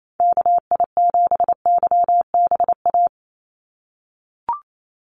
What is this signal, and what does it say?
Morse code, heard as a keyed tone, about 28 wpm, 700 Hz. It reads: KI7YBA
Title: USA Call Signs 28wpm